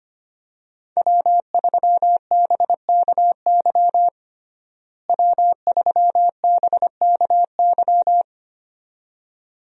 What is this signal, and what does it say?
Morse code, heard as a keyed tone, about 25 wpm, 700 Hz. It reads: W3BKY W3BKY
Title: USA Call Signs - 5 Characters - Repeated 2x - 25wpm